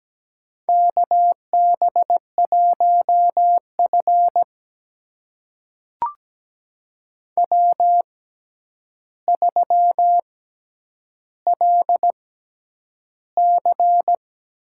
Morse code, heard as a keyed tone, about 17 wpm, spaced as for 9 wpm, 700 Hz. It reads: KB1F W3LC
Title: USA Call Signs - 4 Characters - 6x Character Spacing - 17wpm